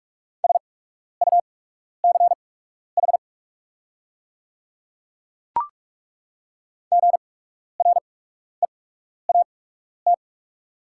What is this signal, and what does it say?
Morse code, heard as a keyed tone, about 43 wpm, spaced as for 12 wpm, 700 Hz. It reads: SUCH GREAT
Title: 8x Character Spacing - Top 300 Words - 45wpm